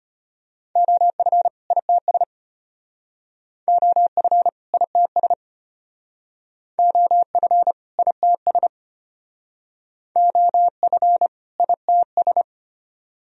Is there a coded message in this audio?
Yes: Morse code from 0.8 to 12.4 s, heard as a keyed tone, about 34 wpm, 700 Hz. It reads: OF ITS OF ITS OF ITS OF ITS